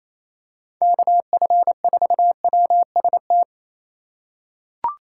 Morse code, heard as a keyed tone, about 28 wpm, 700 Hz. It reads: KF4WST
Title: USA Call Signs - 6 Characters - 3x Character Spacing - 28wpm